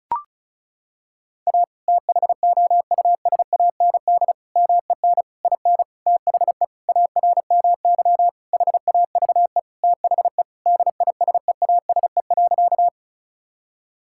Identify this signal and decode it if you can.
Morse code, heard as a keyed tone, about 36 wpm, 700 Hz. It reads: A THOUSAND MEN IN THE ARMY HAVE THE DISEASE.
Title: Sentences from Top 1000 Words - Review Part 6 - 35wpm